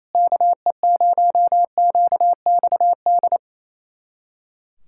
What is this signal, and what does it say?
Morse code, heard as a keyed tone, about 27 wpm, 700 Hz. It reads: KE0QXD
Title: USA Callsign DE Callsign - 6 Characters - 28wpm